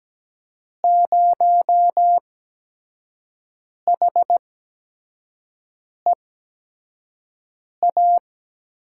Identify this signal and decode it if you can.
Morse code, heard as a keyed tone, about 17 wpm, spaced as for 5 wpm, 700 Hz. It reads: 0HEA
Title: USA Call Signs - 6 Characters - 8x Character Spacing - 17wpm